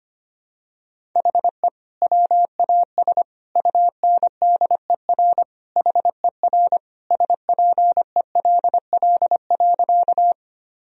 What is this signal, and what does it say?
Morse code, heard as a keyed tone, about 25 wpm, 700 Hz. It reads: HE WAS UNDER HER SPELL.
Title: Sentences from Top 1000 Words - Review Part 1 - 25wpm